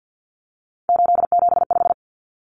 Morse code, heard as a keyed tone, about 50 wpm, 700 Hz. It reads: 875
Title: Numbers - 3-Digit 50wpm